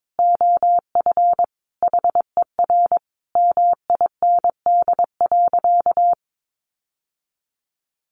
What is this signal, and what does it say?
Morse code, heard as a keyed tone, about 22 wpm, 700 Hz. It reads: OF HER MIND.
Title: Sentences from Top 1300 Words 22wpm